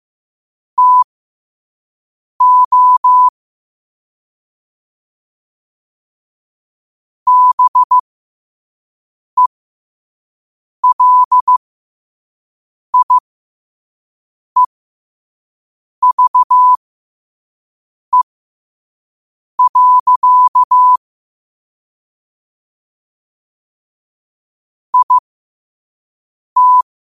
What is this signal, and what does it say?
Morse code, heard as a keyed tone, about 15 wpm, spaced as for 5 wpm, 1000 Hz. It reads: TO BELIEVE. IT
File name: qotdfiles / 5wpm